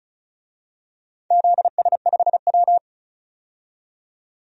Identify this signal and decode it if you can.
Morse code, heard as a keyed tone, about 36 wpm, 700 Hz. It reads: ZS5W
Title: DXCC-100 Call Signs - 4 Characters - 35wpm